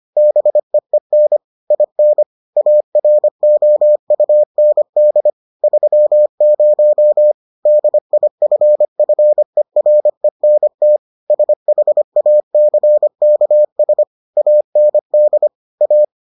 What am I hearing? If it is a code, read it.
Morse code, heard as a keyed tone, about 25 wpm, 600 Hz. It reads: BEEN IN AROUND 30 DIFFERENT SHACKS AND A